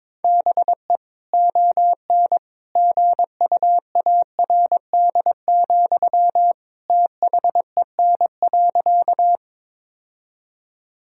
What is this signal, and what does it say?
Morse code, heard as a keyed tone, about 22 wpm, 700 Hz. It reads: BE ON GUARD, THEN.
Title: Sentences from Top 1000 Words - Review Part 4 - 22wpm